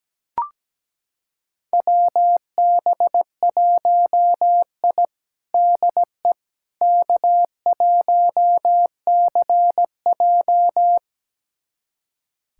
Morse code, heard as a keyed tone, about 17 wpm, 700 Hz. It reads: WB1I DE K1CJ
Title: USA Callsign DE Callsign - 4 Characters - 17wpm